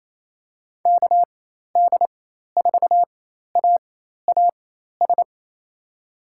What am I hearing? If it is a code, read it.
Morse code, heard as a keyed tone, about 28 wpm, spaced as for 13 wpm, 700 Hz. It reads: KD4AAS
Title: USA Call Signs - 4x Character Spacing - 28wpm